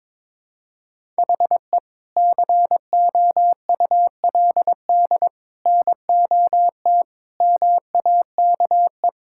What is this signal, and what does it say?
Morse code, heard as a keyed tone, about 22 wpm, 700 Hz. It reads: HE COULD NOT MAKE
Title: Sentences from Top 1600 Words 22wpm